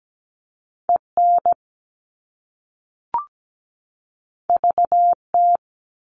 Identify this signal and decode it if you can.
Morse code, heard as a keyed tone, about 17 wpm, 700 Hz. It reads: EN VT
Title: All Cut Numbers - 2-Digit 17wpm